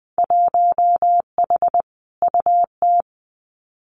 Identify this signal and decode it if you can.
Morse code, heard as a keyed tone, about 20 wpm, 700 Hz. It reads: 1H UT
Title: Winter Field Day Exchanges 20wpm